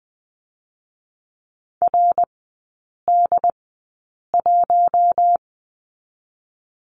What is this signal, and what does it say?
Morse code, heard as a keyed tone, about 19 wpm, spaced as for 8 wpm, 700 Hz. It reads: RD1
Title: 218 Lesson 35e - Number 0 - Three Characters 20wpm